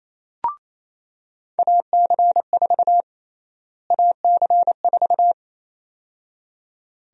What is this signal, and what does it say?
Morse code, heard as a keyed tone, about 28 wpm, 700 Hz. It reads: AC4 AC4
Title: USA Call Sign Prefixes - Repeated 2x - 28wpm